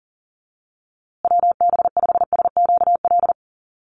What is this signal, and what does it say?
Morse code, heard as a keyed tone, about 40 wpm, 700 Hz. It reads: WB5SQL
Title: USA Call Signs Repeated 3x 40wpm